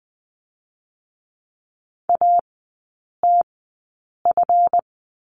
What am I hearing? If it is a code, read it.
Morse code, heard as a keyed tone, about 19 wpm, spaced as for 8 wpm, 700 Hz. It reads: ATF
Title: 166 Lesson 27e - Number 7 - Three Characters 20wpm